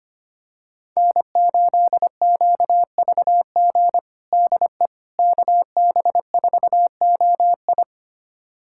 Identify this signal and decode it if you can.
Morse code, heard as a keyed tone, about 25 wpm, 700 Hz. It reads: N8QVG DE KB4OI
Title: USA Callsign DE Callsign - 5 Characters - 25wpm